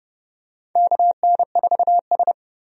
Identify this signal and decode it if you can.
Morse code, heard as a keyed tone, about 30 wpm, 700 Hz. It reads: KN4S
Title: USA Call Signs - 4 Characters - 30wpm